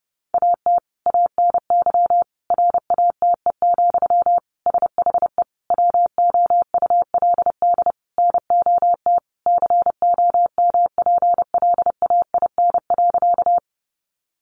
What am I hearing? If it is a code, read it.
Morse code, heard as a keyed tone, about 30 wpm, 700 Hz. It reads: AT ANY RATE, SHE WOULD NOT COMPLAIN.
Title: Sentences from Top 1300 Words 30wpm